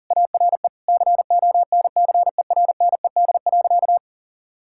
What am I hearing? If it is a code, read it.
Morse code, heard as a keyed tone, about 40 wpm, 700 Hz. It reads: ARE CONCERNED.
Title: Sentences from Top 1000 Words - Review Part 6 - 40wpm